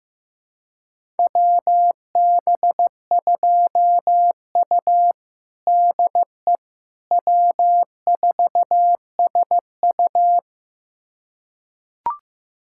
Morse code, heard as a keyed tone, about 15 wpm, 700 Hz. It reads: WB2U DE W4SU
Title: USA Callsign DE Callsign - 4 Characters - 15wpm